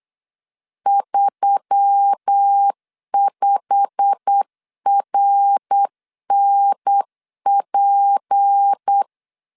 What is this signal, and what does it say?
Morse code, heard as a keyed tone, about 8 wpm, 800 Hz. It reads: 35RNP
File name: CW / DET / 1 9 1500C S A 8wpm 5637